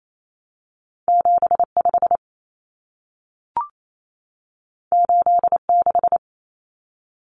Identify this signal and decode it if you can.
Morse code, heard as a keyed tone, about 28 wpm, 700 Hz. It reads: 75 86
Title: Numbers - 2-Digit 28wpm